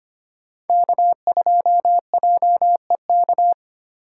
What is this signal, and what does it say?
Morse code, heard as a keyed tone, about 25 wpm, 700 Hz. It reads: K2JEK
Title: USA Call Signs - 5 Characters - 3x Character Spacing - 25wpm